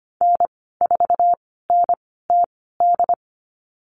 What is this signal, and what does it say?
Morse code, heard as a keyed tone, about 25 wpm, spaced as for 16 wpm, 700 Hz. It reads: N4NTD
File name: Lesson 15g - Number 5 - Call Signs 25wpm